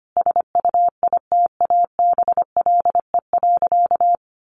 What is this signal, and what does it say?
Morse code, heard as a keyed tone, about 25 wpm, 700 Hz. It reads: SUITABLE.
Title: Sentences from Top 1400 Words 25wpm